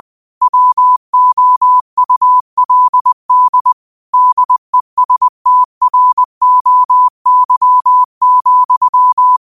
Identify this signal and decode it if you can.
Morse code, heard as a keyed tone, about 20 wpm, 1000 Hz. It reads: WOULD DESTROY,
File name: qotdfiles / 20wpm